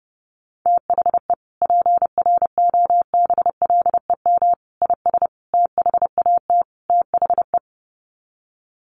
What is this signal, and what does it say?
Morse code, heard as a keyed tone, about 30 wpm, 700 Hz. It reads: THE PROBLEM IS THAT THE
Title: Sets of 5 Words Encore2 30wpm